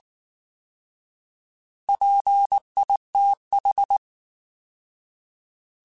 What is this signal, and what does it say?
Morse code, heard as a keyed tone, about 19 wpm, 800 Hz. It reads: PITH
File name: Alphabetical / Alphabetical mp3s / Ps @18